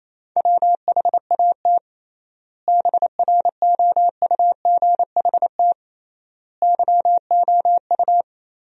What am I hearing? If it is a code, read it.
Morse code, heard as a keyed tone, about 28 wpm, 700 Hz. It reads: WHAT BROUGHT YOU
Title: 3x Word-Spacing - Sentences from Top 500 Words - 28wpm